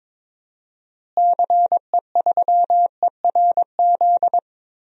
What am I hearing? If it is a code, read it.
Morse code, heard as a keyed tone, about 22 wpm, 700 Hz. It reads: CE3ERZ
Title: DXCC-100 Call Signs - 6 Characters - Repeated 3x - 22wpm